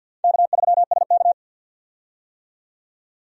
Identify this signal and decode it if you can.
Morse code, heard as a keyed tone, about 47 wpm, 700 Hz. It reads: K3SK
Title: USA Call Signs - 4 Characters - 50wpm